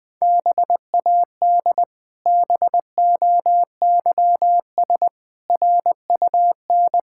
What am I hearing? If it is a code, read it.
Morse code, heard as a keyed tone, about 20 wpm, 700 Hz. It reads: BAD BOYS RUN
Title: Pangram Challenge v2 - 20wpm